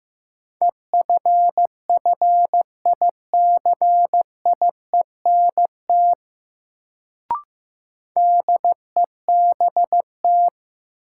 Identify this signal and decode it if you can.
Morse code, heard as a keyed tone, about 15 wpm, 700 Hz. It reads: EFFICIENT DEBT
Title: Sentences from Top 1300 Words - New Word Drill - 15wpm